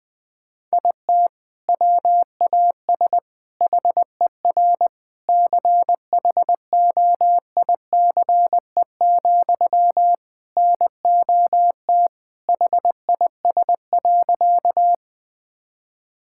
Morse code, heard as a keyed tone, about 20 wpm, 700 Hz. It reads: IT WAS HER CHOICE, NOT HIS.
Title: Sentences from Top 1000 Words - Review Part 6 - 20wpm